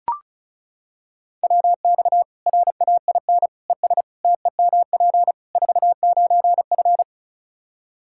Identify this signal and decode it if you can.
Morse code, heard as a keyed tone, about 36 wpm, 700 Hz. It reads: WX RAIN ES TEMP 49F
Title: Practice QSO - Exchange 2 - Info - 35wpm